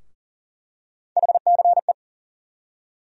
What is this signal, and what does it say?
Morse code, heard as a keyed tone, about 40 wpm, 700 Hz. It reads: HCE
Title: USA Call Sign Postfixes - 3 Letters 40wpm